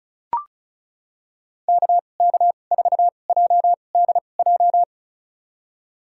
Morse code, heard as a keyed tone, about 36 wpm, spaced as for 25 wpm, 700 Hz. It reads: KK4JDJ
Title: USA Call Signs - 6 Characters - 2x Character Spacing - 35wpm